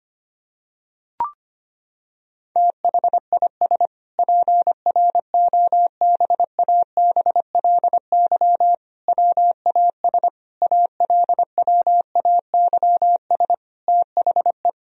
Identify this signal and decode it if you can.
Morse code, heard as a keyed tone, about 25 wpm, 700 Hz. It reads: THIS PROBABLY WAS ALWAYS THE
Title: Sentences from Top 1000 Words - Review Part 4 - 25wpm